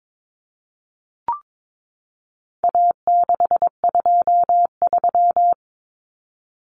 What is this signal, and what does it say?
Morse code, heard as a keyed tone, about 22 wpm, 700 Hz. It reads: A623
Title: Common Cut Numbers - 4-Digit 22wpm